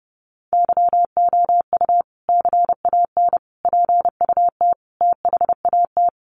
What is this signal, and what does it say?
Morse code, heard as a keyed tone, about 30 wpm, 700 Hz. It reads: YOU CAN PUT THAT
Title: Sentences from Top 1500 Words 30wpm